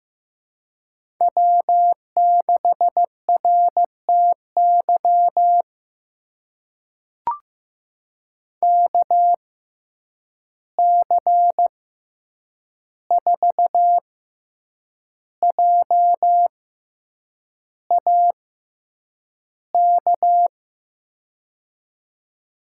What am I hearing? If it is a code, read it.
Morse code, heard as a keyed tone, about 15 wpm, spaced as for 7 wpm, 700 Hz. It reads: W6RTY KC4JAK
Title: USA Call Signs - 6x Character Spacing - 15wpm